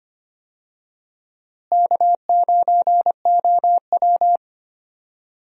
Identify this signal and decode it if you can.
Morse code, heard as a keyed tone, about 25 wpm, 700 Hz. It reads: K9OW
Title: USA Call Signs - 4 Characters - Repeated 3x - 25wpm